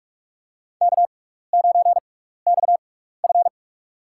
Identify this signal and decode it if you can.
Morse code, heard as a keyed tone, about 46 wpm, spaced as for 16 wpm, 700 Hz. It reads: K9XF
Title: USA Call Signs - 4 Characters - 6x Character Spacing - 45wpm